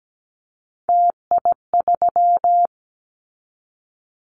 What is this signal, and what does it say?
Morse code, heard as a keyed tone, about 17 wpm, 700 Hz. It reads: TI3
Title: Callsign Prefixes for Top 100 DXCC - 17wpm